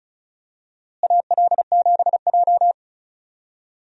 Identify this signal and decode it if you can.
Morse code, heard as a keyed tone, about 36 wpm, 700 Hz. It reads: AL7J
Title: USA Call Signs - 4 Characters - Repeated 2x - 35wpm